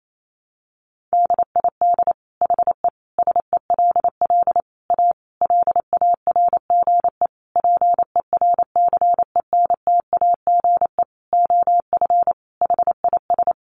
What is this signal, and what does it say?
Morse code, heard as a keyed tone, about 28 wpm, 700 Hz. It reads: DID HE SELL A LARGE PERCENTAGE OF HIS
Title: Sentences from Top 1300 Words 28wpm